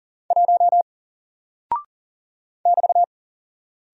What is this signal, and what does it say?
Morse code, heard as a keyed tone, about 40 wpm, 700 Hz. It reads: 1 <BT>
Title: Single Character (Letter:Number:Punctuation:Prosign) Alt 40wpm